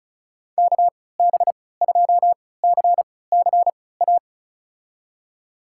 Morse code, heard as a keyed tone, about 36 wpm, spaced as for 20 wpm, 700 Hz. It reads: KB2CCA
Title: USA Call Signs - 6 Characters - 3x Character Spacing - 35wpm